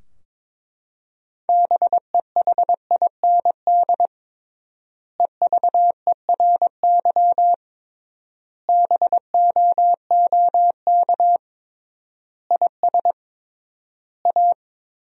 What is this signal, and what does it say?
Morse code, heard as a keyed tone, about 21 wpm, 700 Hz. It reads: BEHIND EVERY BOOK IS A
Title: 3x Word-Spacing - Sentences from Top 500 Words - 22wpm